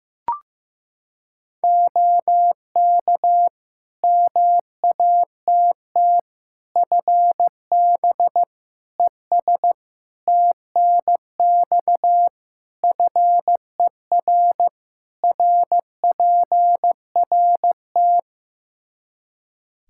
Morse code, heard as a keyed tone, about 15 wpm, 700 Hz. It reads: OK MATT FB ES TNX FER RPRT
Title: Practice QSO - Exchange 2 - Info - 15wpm